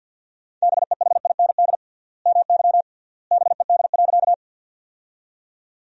Morse code, heard as a keyed tone, about 50 wpm, 700 Hz. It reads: BEHIND MY BED.
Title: 3x Word-Spacing - Sentences from Top 500 Words - 50wpm